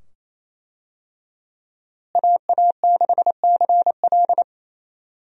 Morse code, heard as a keyed tone, about 28 wpm, 700 Hz. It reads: AA6CL
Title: USA Call Signs - 5 Characters - Repeated 2x - 28wpm